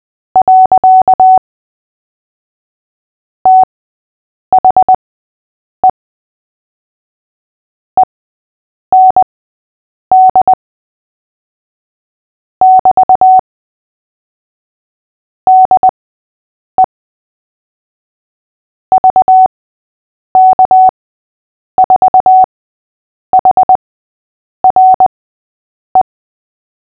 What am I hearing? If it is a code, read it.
Morse code, heard as a keyed tone, about 20 wpm, spaced as for 8 wpm, 750 Hz. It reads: . THE END <BT> DE VK4HRE
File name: cw / fists / Short-Story-5-8wpm